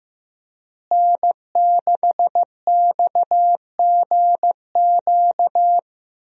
Morse code, heard as a keyed tone, about 15 wpm, 700 Hz. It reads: N6XGQ
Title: USA Call Signs - 5 Characters - Repeated 3x - 15wpm